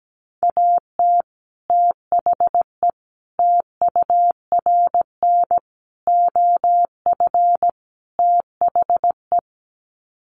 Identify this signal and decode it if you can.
Morse code, heard as a keyed tone, about 17 wpm, 700 Hz. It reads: AT THE TURN OF THE
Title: Sets of 5 Words 17wpm